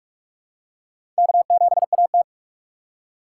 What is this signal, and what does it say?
Morse code, heard as a keyed tone, about 44 wpm, 700 Hz. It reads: K7AT
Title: USA Call Signs - 4 Characters - 4x Character Spacing - 45wpm